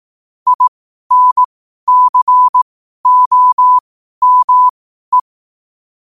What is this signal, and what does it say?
Morse code, heard as a keyed tone, about 18 wpm, spaced as for 12 wpm, 1000 Hz. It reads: INCOME
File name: qotdfiles / 12wpm